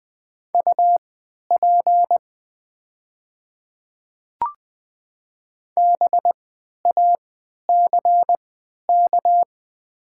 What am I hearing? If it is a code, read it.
Morse code, heard as a keyed tone, about 20 wpm, spaced as for 11 wpm, 700 Hz. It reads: UP BACK
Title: 3x Character Spacing - Top 200 Words - 20wpm